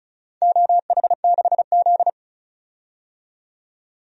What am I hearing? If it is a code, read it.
Morse code, heard as a keyed tone, about 36 wpm, 700 Hz. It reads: OH6Z
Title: DXCC-100 Call Signs - 4 Characters - 35wpm